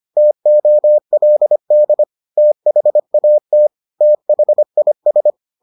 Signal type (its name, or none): Morse code